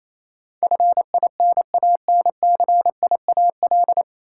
Morse code, heard as a keyed tone, about 28 wpm, 700 Hz. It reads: FINANCIAL
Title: Sentences from Top 1000 Words - Review Part 5 - New Word Drill - 28wpm